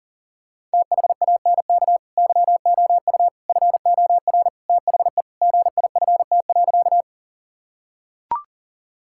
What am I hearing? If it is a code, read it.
Morse code, heard as a keyed tone, about 37 wpm, 700 Hz. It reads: THANK YOU FOR THE GIFT.
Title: Sentences from Top 1000 Words - Review Part 6 - 40wpm